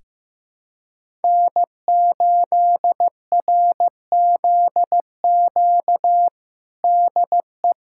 Morse code, heard as a keyed tone, about 15 wpm, 700 Hz. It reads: N8RZQ DE
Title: USA Callsign DE Callsign - 5 Characters - 15wpm